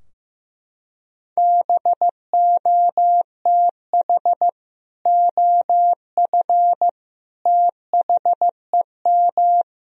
Morse code, heard as a keyed tone, about 15 wpm, 700 Hz. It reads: BOTH OF THEM
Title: Sentences from Top 1000 Words - Review Part 2 - 15wpm